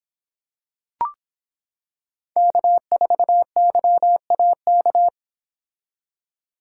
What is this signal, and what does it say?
Morse code, heard as a keyed tone, about 26 wpm, 700 Hz. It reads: K4YAK